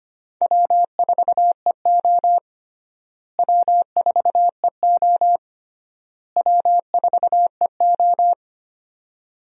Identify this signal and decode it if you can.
Morse code, heard as a keyed tone, about 25 wpm, 700 Hz. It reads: W4EO W4EO W4EO
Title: USA Call Signs - 4 Characters - Repeated 3x - 25wpm